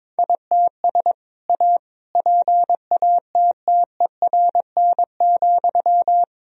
Morse code, heard as a keyed tone, about 22 wpm, 700 Hz. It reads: ITS A PATTERN,
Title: Sentences from Top 1000 Words - Review Part 2 - 22wpm